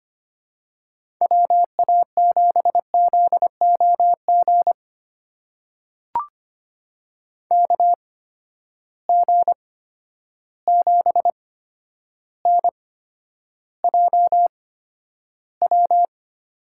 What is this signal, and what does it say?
Morse code, heard as a keyed tone, about 25 wpm, spaced as for 11 wpm, 700 Hz. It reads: WA7ZOG KG7NJW
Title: USA Call Signs - 6 Characters - 8x Character Spacing - 25wpm